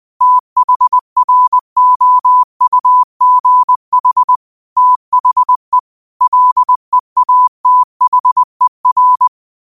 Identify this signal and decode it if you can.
Morse code, heard as a keyed tone, about 20 wpm, 1000 Hz. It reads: THROUGH THE LEATHER